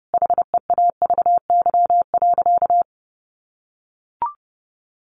Morse code, heard as a keyed tone, about 30 wpm, 700 Hz. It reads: HEAVY.
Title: Sentences from Top 1100 Words 30wpm